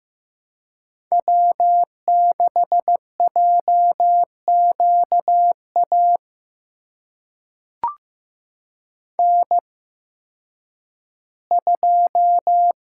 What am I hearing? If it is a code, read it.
Morse code, heard as a keyed tone, about 15 wpm, spaced as for 10 wpm, 700 Hz. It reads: W6JQA N2
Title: USA Call Signs - 5 Characters - 8x Character Spacing - 15wpm